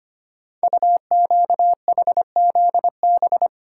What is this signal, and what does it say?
Morse code, heard as a keyed tone, about 25 wpm, 700 Hz. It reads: UQHZB
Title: Code Groups - 5 Letters - 25wpm